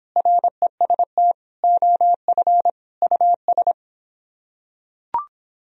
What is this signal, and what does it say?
Morse code, heard as a keyed tone, about 25 wpm, 700 Hz. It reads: REST OF US